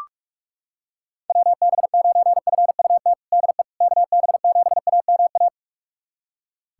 Morse code, heard as a keyed tone, about 41 wpm, 690 Hz. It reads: WB9FUT DE KB7AMA
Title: USA Callsign DE Callsign - 6 Characters - 45wpm